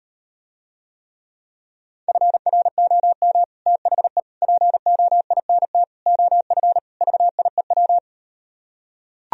Morse code, heard as a keyed tone, about 38 wpm, 700 Hz. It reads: FROM THE POINT OF VIEW